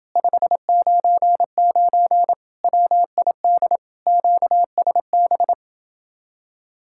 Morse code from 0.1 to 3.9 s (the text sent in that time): 599 WID